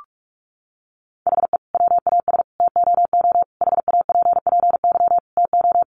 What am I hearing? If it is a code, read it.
Morse code, heard as a keyed tone, about 44 wpm, 700 Hz. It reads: HE WAS TOO HAPPY TO
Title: Sentences from Top 1000 Words 45wpm